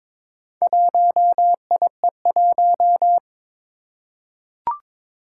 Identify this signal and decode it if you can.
Morse code, heard as a keyed tone, about 22 wpm, 700 Hz. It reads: 1IE1
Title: Four Challenging Letter-Number Combinations 22wpm